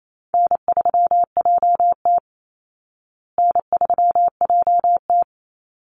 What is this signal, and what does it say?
Morse code, heard as a keyed tone, about 28 wpm, 700 Hz. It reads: N3JT N3JT
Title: Call Signs Repeated 2x - 28wpm